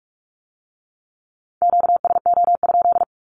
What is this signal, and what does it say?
Morse code, heard as a keyed tone, about 46 wpm, 700 Hz. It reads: QSO?
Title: Q-Codes with Spoken Meaning - 45wpm